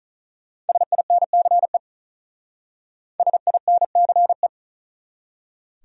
Morse code, heard as a keyed tone, about 46 wpm, 700 Hz. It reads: SINCE SINCE